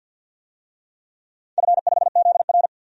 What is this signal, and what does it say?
Morse code, heard as a keyed tone, about 46 wpm, 700 Hz. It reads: F5ZR
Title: DXCC-100 Call Signs - 4 Characters - 50wpm